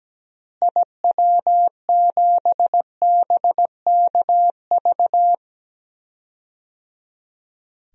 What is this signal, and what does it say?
Morse code, heard as a keyed tone, about 17 wpm, 700 Hz. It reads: IW7BKV
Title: DXCC-100 Call Signs - 6 Characters - Repeated 3x - 17wpm